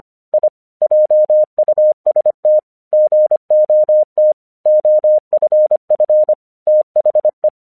Morse code, heard as a keyed tone, about 25 wpm, 610 Hz. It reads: I JUST GOT OFF THE
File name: 1 English 25wpm / AQC03